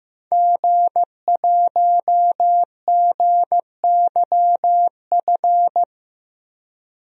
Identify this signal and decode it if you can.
Morse code, heard as a keyed tone, about 15 wpm, 700 Hz. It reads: G1GYF
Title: DXCC-100 Call Signs - 5 Characters - 15wpm